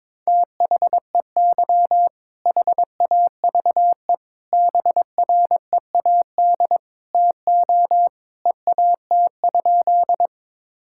Morse code, heard as a keyed tone, about 22 wpm, 700 Hz. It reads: THEY HAVE BREAD TO EAT?
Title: Sentences from Top 1600 Words 22wpm